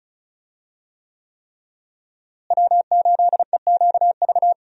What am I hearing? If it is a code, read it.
Morse code, heard as a keyed tone, about 36 wpm, 700 Hz. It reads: W8EQV
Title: USA Call Signs - 5 Characters - Repeated 3x - 35wpm